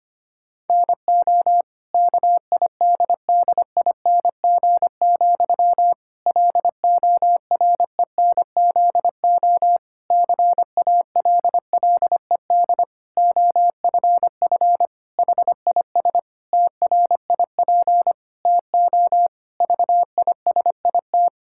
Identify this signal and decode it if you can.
Morse code, heard as a keyed tone, about 25 wpm, 700 Hz. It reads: NO KIDDING, LORENZO CALLED OFF HIS TRIP TO VISIT
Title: Pangram Challenge v2 - 25wpm